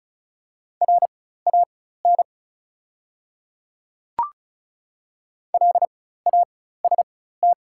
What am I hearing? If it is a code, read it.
Morse code, heard as a keyed tone, about 34 wpm, spaced as for 16 wpm, 700 Hz. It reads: RAN LAST
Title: 4x Character Spacing - Top 500 Words - 35wpm